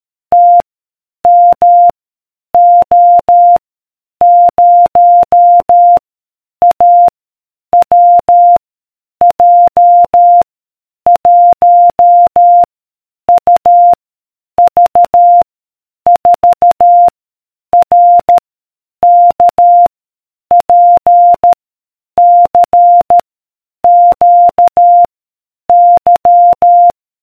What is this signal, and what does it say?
Morse code, heard as a keyed tone, about 13 wpm, 700 Hz. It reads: T M O 0 A W J 1 U V 4 R K P C Q Y